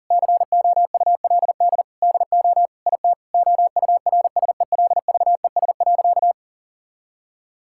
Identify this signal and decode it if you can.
Morse code, heard as a keyed tone, about 40 wpm, 700 Hz. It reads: COULD DO IT OURSELVES.
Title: Sentences from Top 1600 Words 40wpm